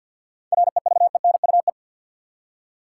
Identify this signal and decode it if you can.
Morse code, heard as a keyed tone, about 50 wpm, 700 Hz. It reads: REVENUE
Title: Seven Letter Words - 50wpm